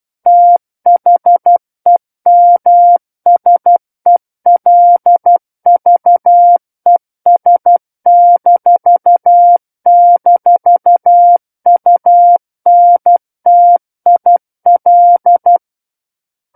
Morse code, heard as a keyed tone, about 12 wpm, 700 Hz. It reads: THEMSELVES--UNTIL
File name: Book One / 03-Chapter 3 - 12 WPM